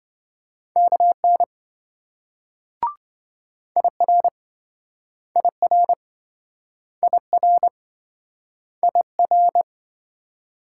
Morse code, heard as a keyed tone, about 27 wpm, 700 Hz. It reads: KN IR IR IR IR